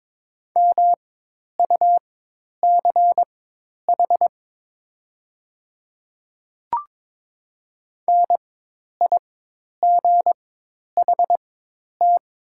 Morse code, heard as a keyed tone, about 22 wpm, spaced as for 10 wpm, 700 Hz. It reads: MUCH NIGHT
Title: 4x Character Spacing - Top 300 Words - 22wpm